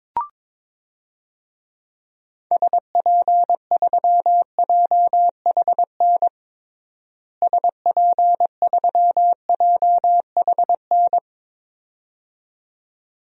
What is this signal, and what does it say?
Morse code, heard as a keyed tone, about 22 wpm, 700 Hz. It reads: SP3JHN SP3JHN
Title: DXCC-100 Call Signs - 6 Characters - Repeated 2x - 22wpm